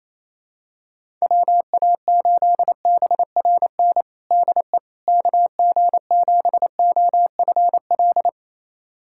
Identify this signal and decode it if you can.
Morse code, heard as a keyed tone, about 28 wpm, 700 Hz. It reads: WA8BRN DE KG7OFL
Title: USA Callsign DE Callsign - 6 Characters - 28wpm